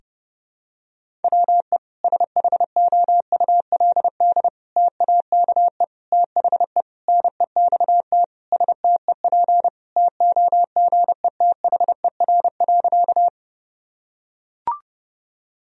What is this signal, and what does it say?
Morse code, heard as a keyed tone, about 32 wpm, 700 Hz. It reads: WE SHOULD TAKE THE NEXT STEP TOGETHER.
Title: Sentences from Top 1000 Words - Review Part 2 - 30wpm